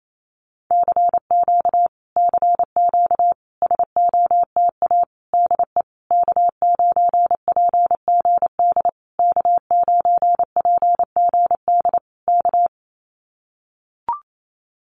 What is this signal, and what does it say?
Morse code, heard as a keyed tone, about 28 wpm, 700 Hz. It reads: CQ CQ SOTA DE K9PGD K9PGD K
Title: Common QSO Phrases Encore 28wpm